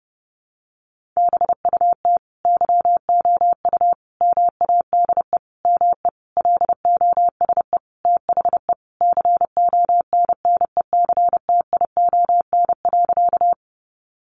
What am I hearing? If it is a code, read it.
Morse code, heard as a keyed tone, about 30 wpm, 700 Hz. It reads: BUT YOU MADE ME LOSE THE CONNECTION.
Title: Sentences from Top 1300 Words 30wpm